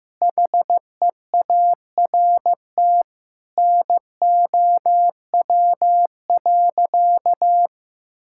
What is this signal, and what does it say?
Morse code, heard as a keyed tone, about 15 wpm, 700 Hz. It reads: HEART NOW.
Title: Sentences from Top 1600 Words 15wpm